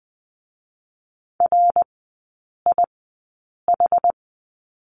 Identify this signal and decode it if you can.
Morse code, heard as a keyed tone, about 20 wpm, spaced as for 8 wpm, 700 Hz. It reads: RIH
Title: 159 Lesson 26e - Letter g - Three Characters 20wpm